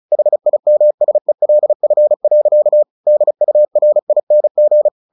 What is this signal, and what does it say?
Morse code, heard as a keyed tone, about 34 wpm, 600 Hz. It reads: HIMSELF. DURING
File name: the old man and the sea / the old man and the sea 1 35wpm